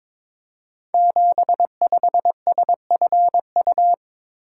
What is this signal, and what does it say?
Morse code, heard as a keyed tone, about 22 wpm, 700 Hz. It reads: 75SFU
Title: Code Groups - 5 Letter:Numbers - 22wpm